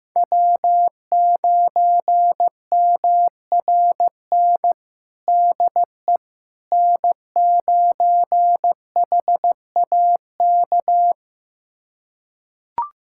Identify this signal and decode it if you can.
Morse code, heard as a keyed tone, about 16 wpm, 700 Hz. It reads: W9MRN DE N9HAK
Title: USA Callsign DE Callsign - 5 Characters - 15wpm